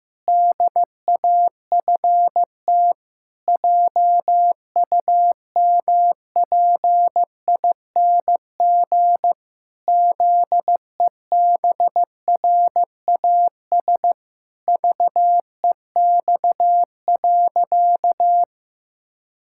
Morse code, heard as a keyed tone, about 15 wpm, 700 Hz. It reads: DAFT JUMPING ZEBRAS VEX.
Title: Pangram Challenge v2 - 15wpm